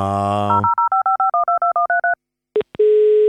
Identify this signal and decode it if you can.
DTMF, heard as telephone touch tones, tones of about 270 ms. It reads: 051213